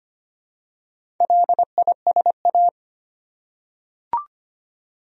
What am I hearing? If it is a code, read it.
Morse code, heard as a keyed tone, about 25 wpm, 700 Hz. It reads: LISA
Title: USA Common Names - Max 4 Letters - 25wpm